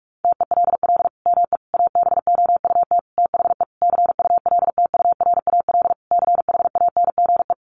Morse code, heard as a keyed tone, about 44 wpm, 700 Hz. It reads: TELL ME ABOUT THE CULTURAL CHANGE
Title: Sentences from Top 1500 Words 45wpm